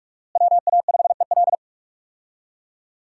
Morse code, heard as a keyed tone, about 46 wpm, 700 Hz. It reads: WA5EL
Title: USA Call Signs - 5 Characters - Repeated 3x - 45wpm